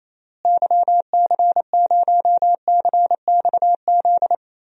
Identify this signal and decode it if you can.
Morse code, heard as a keyed tone, about 28 wpm, 700 Hz. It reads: YC0CXZ
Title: DXCC-100 Call Signs - 6 Characters - Repeated 2x - 28wpm